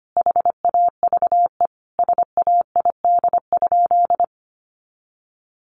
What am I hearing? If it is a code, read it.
Morse code, heard as a keyed tone, about 25 wpm, 700 Hz. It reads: HAVE SAID?
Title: Sentences from Top 100 Words 25wpm